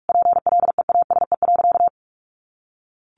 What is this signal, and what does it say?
Morse code, heard as a keyed tone, about 44 wpm, 700 Hz. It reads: PLEASE.
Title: Sentences from Top 1500 Words 45wpm